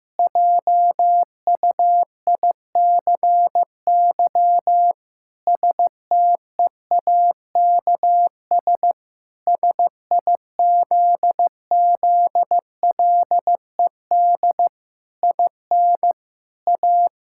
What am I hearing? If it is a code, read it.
Morse code, heard as a keyed tone, about 15 wpm, 700 Hz. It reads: JUICY STEAKS SIZZLED IN A
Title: Pangram Challenge v2 - 15wpm